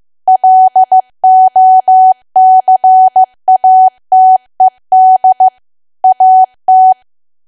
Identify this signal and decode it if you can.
Morse code, heard as a keyed tone, about 15 wpm, 750 Hz. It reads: LOCATED AT
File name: cwa / W1AW-Level2 / 15WPM